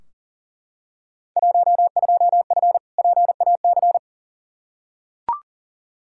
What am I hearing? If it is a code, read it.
Morse code, heard as a keyed tone, about 37 wpm, 700 Hz. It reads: 12F PAC
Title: Field Day Exchanges Encore - 40wpm